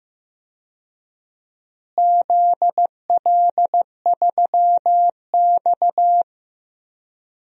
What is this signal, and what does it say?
Morse code, heard as a keyed tone, about 15 wpm, 700 Hz. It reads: ZL3X
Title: DXCC-100 Call Signs - 4 Characters - Repeated 2x - 15wpm